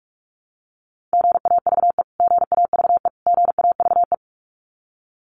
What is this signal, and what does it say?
Morse code, heard as a keyed tone, about 44 wpm, 700 Hz. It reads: GAVE GAVE GAVE
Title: Top 500 Words Repeated 45wpm